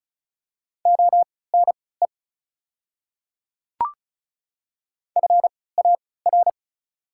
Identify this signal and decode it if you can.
Morse code, heard as a keyed tone, about 36 wpm, spaced as for 20 wpm, 700 Hz. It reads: ONE FAR
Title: 3x Character Spacing - Top 300 Words - 35wpm